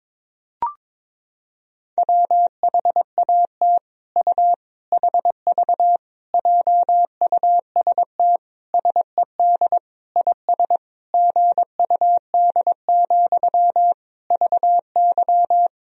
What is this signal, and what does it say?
Morse code, heard as a keyed tone, about 22 wpm, 700 Hz. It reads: WHAT U HV JUST SED IS GUD, VY
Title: Sentences from Top 200 Words and Abbreviations 22wpm